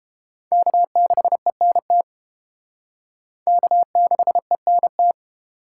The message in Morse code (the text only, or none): K6ENT K6ENT